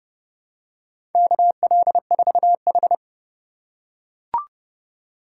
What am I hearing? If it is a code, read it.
Morse code, heard as a keyed tone, about 30 wpm, 700 Hz. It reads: KL4H
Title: USA Call Signs - 4 Characters - 4x Character Spacing - 30wpm